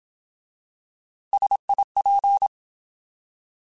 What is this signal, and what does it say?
Morse code, heard as a keyed tone, about 27 wpm, 800 Hz. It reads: SIP
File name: Alphabetical / Alphabetical mp3s / Ss @25